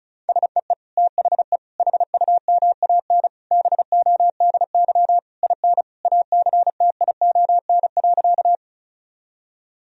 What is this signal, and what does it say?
Morse code, heard as a keyed tone, about 34 wpm, 700 Hz. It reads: SEE THE HUMAN BODY IN ACTION.
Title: Sentences from Top 1000 Words - Review Part 4 - 35wpm